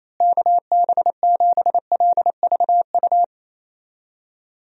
Morse code, heard as a keyed tone, about 27 wpm, 700 Hz. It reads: KB7LVU
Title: USA Call Signs - 6 Characters - Repeated 3x - 28wpm